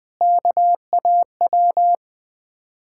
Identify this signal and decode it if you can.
Morse code, heard as a keyed tone, about 20 wpm, 700 Hz. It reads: KAW
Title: Ontario QSO Party Exchange 20wpm